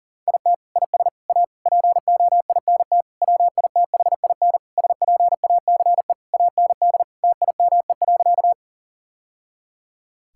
Morse code, heard as a keyed tone, about 40 wpm, 700 Hz. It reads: IT IS A POINT WITHIN SPACE AND TIME.
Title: Sentences from Top 1000 Words - Review Part 5 - 40wpm